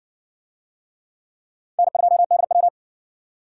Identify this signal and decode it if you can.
Morse code, heard as a keyed tone, about 57 wpm, 700 Hz. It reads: N2DW